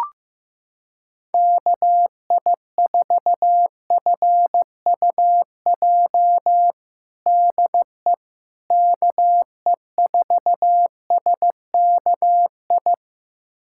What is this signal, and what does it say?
Morse code, heard as a keyed tone, about 15 wpm, 700 Hz. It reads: KI4FUJ DE KE4SKI
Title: USA Callsign DE Callsign - 6 Characters - 15wpm